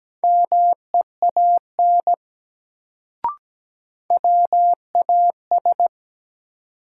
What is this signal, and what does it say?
Morse code, heard as a keyed tone, about 17 wpm, 700 Hz. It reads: MEAN WAS
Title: Sets of 2 Words - New Word Drill - 17wpm